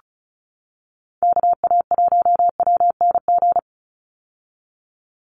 Morse code, heard as a keyed tone, about 34 wpm, 700 Hz. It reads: KA1WNG
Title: USA Call Signs 35wpm